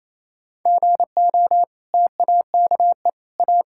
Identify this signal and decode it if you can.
Morse code, heard as a keyed tone, about 28 wpm, 700 Hz. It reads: GO TAKE A
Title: Sentences from Top 1600 Words 28wpm